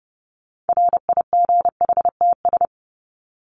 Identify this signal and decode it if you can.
Morse code, heard as a keyed tone, about 30 wpm, 700 Hz. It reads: RIGHTS
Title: ES:S Suffix - Top 100 Words with Base Words 30wpm